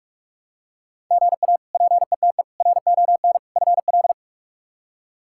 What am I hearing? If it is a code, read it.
Morse code, heard as a keyed tone, about 43 wpm, 700 Hz. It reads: GA PETE RON FL
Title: K1USN SST QSO Practice - 45wpm